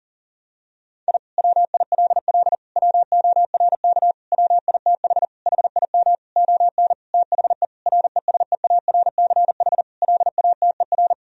Morse code, heard as a keyed tone, about 40 wpm, 700 Hz. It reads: I WILL WORK WITH HIM ON THE RESEARCH LATER
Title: Sentences from Top 1000 Words - Review Part 4 - 40wpm